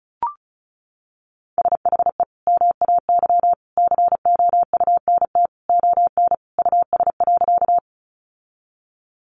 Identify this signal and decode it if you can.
Morse code, heard as a keyed tone, about 34 wpm, 700 Hz. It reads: SHE MAY COUNT ON US.
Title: Sentences from Top 1500 Words 35wpm